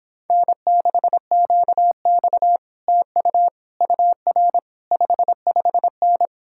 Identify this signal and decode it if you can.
Morse code, heard as a keyed tone, about 26 wpm, 700 Hz. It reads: N6QX TU UR 55N